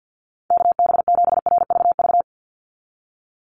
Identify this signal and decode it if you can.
Morse code, heard as a keyed tone, about 50 wpm, 700 Hz. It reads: KB7RVV
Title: USA Call Signs 50wpm